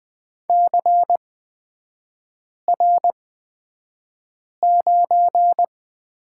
Morse code, heard as a keyed tone, about 20 wpm, spaced as for 5 wpm, 690 Hz. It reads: CR9
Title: 229 Lesson 37e - Letter x - Three Characters 20wpm